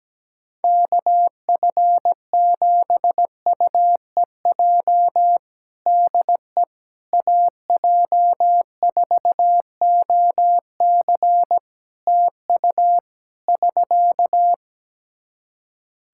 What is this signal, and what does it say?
Morse code, heard as a keyed tone, about 17 wpm, 700 Hz. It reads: KF7UEJ DE AJ4OC TU <SK>
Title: Practice QSO - Exchange 4 - Ending - 17wpm